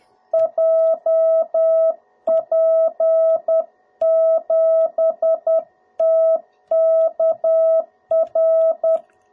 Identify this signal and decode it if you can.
Morse code, heard as a keyed tone, about 10 wpm, 660 Hz. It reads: JP7TKR
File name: jh8cbh / 2021cw / cw02b